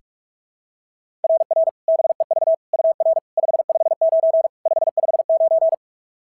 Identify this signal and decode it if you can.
Morse code, heard as a keyed tone, about 44 wpm, 690 Hz. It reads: RR BEV UR 559 559
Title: SOTA QSO Practice 45wpm